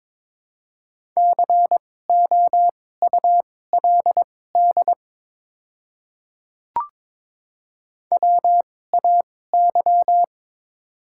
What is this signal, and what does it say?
Morse code, heard as a keyed tone, about 22 wpm, spaced as for 16 wpm, 700 Hz. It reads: COULD WAY
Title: 2x Character Spacing - Top 100 Words - 22wpm